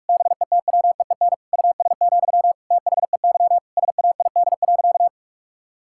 Morse code, heard as a keyed tone, about 47 wpm, 700 Hz. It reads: BETWEEN US, THEY SAID.
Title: Sentences from Top 1600 Words 45wpm